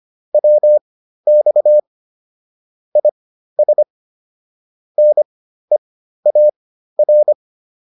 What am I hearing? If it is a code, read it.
Morse code, heard as a keyed tone, about 25 wpm, spaced as for 13 wpm, 600 Hz. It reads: WX IS NEAR